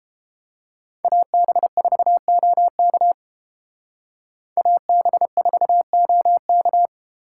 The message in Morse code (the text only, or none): AB4OK AB4OK